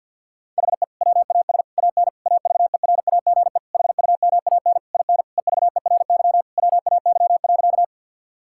Morse code, heard as a keyed tone, about 50 wpm, 700 Hz. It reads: HE WAS AN AVERAGE HUMAN IN EVERY WAY.
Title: Sentences from Top 1000 Words - Review Part 6 - 50wpm